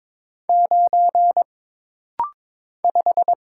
Morse code, heard as a keyed tone, about 22 wpm, 700 Hz. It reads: 9 5
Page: This rapid-fire timing is geared to push your ICR (Instant Character Recognition) skills.